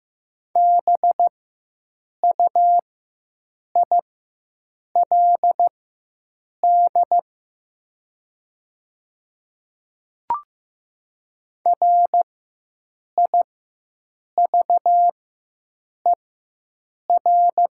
Morse code, heard as a keyed tone, about 15 wpm, spaced as for 7 wpm, 700 Hz. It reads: BUILD RIVER
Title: 4x Character Spacing - Top 300 Words - 15wpm